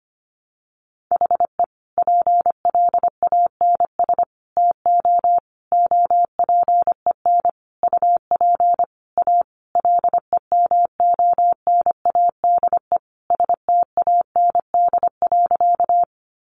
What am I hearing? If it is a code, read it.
Morse code, heard as a keyed tone, about 25 wpm, 700 Hz. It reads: HE PLANS TO OPEN UP A LEMONADE STAND.
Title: Additional Easy Phrases 25wpm